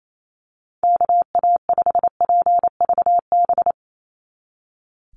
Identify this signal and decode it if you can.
Morse code, heard as a keyed tone, about 28 wpm, 700 Hz. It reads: KA5PVB
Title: NA SOTA Chaser Callsigns 28wpm